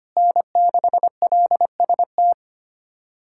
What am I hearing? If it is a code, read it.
Morse code, heard as a keyed tone, about 25 wpm, 700 Hz. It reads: N6LST
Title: USA Callsign DE Callsign 25wpm